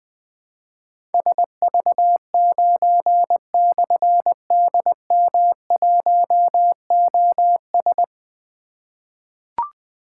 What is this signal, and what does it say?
Morse code, heard as a keyed tone, about 21 wpm, 700 Hz. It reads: SV9/DM1OS
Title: Difficult Call Signs 20wpm